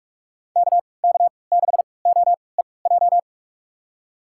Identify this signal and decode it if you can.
Morse code, heard as a keyed tone, about 47 wpm, spaced as for 26 wpm, 700 Hz. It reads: KK6OEJ
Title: USA Call Signs - 3x Character Spacing - 45wpm